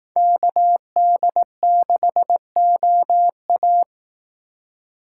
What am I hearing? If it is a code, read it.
Morse code, heard as a keyed tone, about 18 wpm, 700 Hz. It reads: KD6OA
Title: Speed Racing - USA 2x2 Call Signs 15wpm